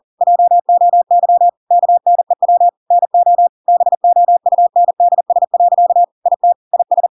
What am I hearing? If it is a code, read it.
Morse code, heard as a keyed tone, about 40 wpm, 700 Hz. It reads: JOY KNEW NO BOUNDS. IT IS
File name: Pinocchio / Pinocchio ch5 40 40 ik1hge